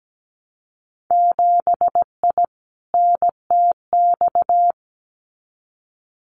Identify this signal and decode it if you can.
Morse code, heard as a keyed tone, about 17 wpm, 700 Hz. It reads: 7I NTX
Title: Winter Field Day Exchanges Encore 17wpm